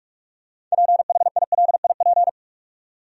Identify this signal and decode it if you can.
Morse code, heard as a keyed tone, about 46 wpm, 700 Hz. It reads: PHILIP
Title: USA Common Names - 45wpm